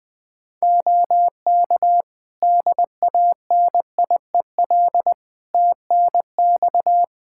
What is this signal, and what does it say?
Morse code, heard as a keyed tone, about 20 wpm, 700 Hz. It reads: OK DANIEL TNX
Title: Practice QSO - Exchange 4 - Ending - 20wpm